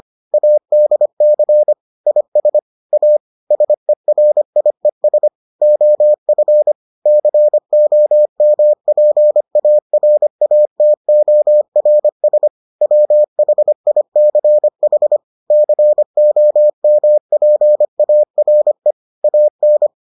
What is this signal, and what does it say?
Morse code, heard as a keyed tone, about 25 wpm, 600 Hz. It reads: ADC IS A SERIES OF COMPARATORS WHICH COMPARE AN